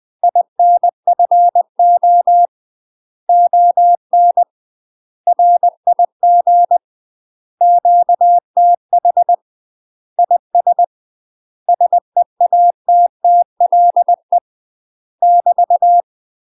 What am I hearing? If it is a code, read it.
Morse code, heard as a keyed tone, about 20 wpm, 700 Hz. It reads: INFO ON RIG QTH IS SEATTLE <BT>
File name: UK MASTER QSOs TRAINING / 20WPM / UK MASTER QSOs 0058